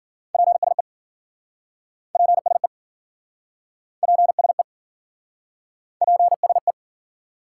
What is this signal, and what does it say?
Morse code, heard as a keyed tone, about 53 wpm, 700 Hz. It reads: PSE PSE PSE PSE